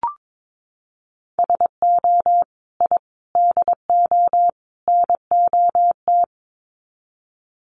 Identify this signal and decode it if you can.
Morse code, heard as a keyed tone, about 22 wpm, 700 Hz. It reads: SO I DO NOT
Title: Sets of 4 Words 22wpm